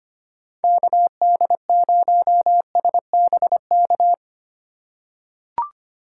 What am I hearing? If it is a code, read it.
Morse code, heard as a keyed tone, about 25 wpm, 700 Hz. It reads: KD0SBK
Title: USA Call Signs - 4x Character Spacing - 25wpm